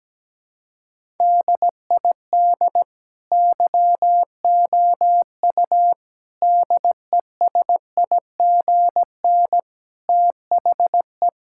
Morse code, heard as a keyed tone, about 17 wpm, 700 Hz. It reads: DID YOU DESIGN THE
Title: Sentences from Top 1000 Words - Review Part 5 - 17wpm